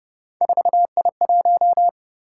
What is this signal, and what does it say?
Morse code, heard as a keyed tone, about 29 wpm, 700 Hz. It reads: 4I1
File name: Speed-Racing - Three Letter-Number Combinations - 25wpm